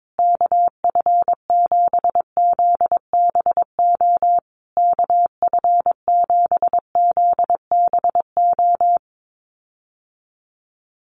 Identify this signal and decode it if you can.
Morse code, heard as a keyed tone, about 22 wpm, 700 Hz. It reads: KF7ZBO KF7ZBO
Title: USA Call Signs Repeated 3x 22wpm